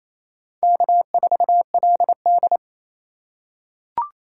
Morse code, heard as a keyed tone, about 28 wpm, 700 Hz. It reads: K4LD
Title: USA Call Signs - 3x Character Spacing - 28wpm